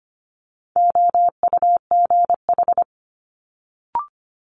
Morse code, heard as a keyed tone, about 25 wpm, 700 Hz. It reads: OUGH
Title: Multi-letter Phonemes Quick Fire 25wpm